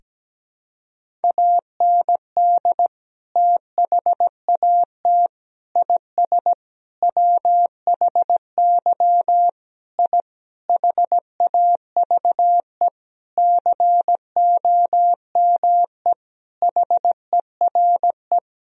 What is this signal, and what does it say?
Morse code, heard as a keyed tone, about 17 wpm, 700 Hz. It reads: AND THAT IS WHY I HAVE COME HERE
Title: Sentences from Top 1000 Words - Review Part 3 - 17wpm